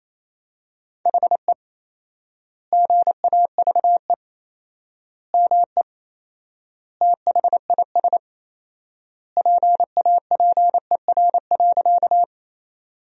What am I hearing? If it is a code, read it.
Morse code, heard as a keyed tone, about 28 wpm, 700 Hz. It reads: HE GAVE ME THIS PAPER.
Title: 4x Word-Spacing - Sentences from Top 500 Words - 28wpm